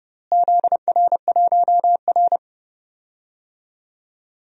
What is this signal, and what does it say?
Morse code, heard as a keyed tone, about 31 wpm, 700 Hz. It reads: ZR1R
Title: DXCC-100 Call Signs - 4 Characters - Repeated 3x - 30wpm